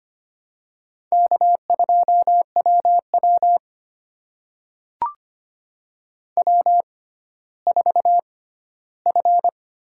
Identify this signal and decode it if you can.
Morse code, heard as a keyed tone, about 25 wpm, spaced as for 14 wpm, 700 Hz. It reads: K2WW W4F
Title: USA Call Signs - 4 Characters - 6x Character Spacing - 25wpm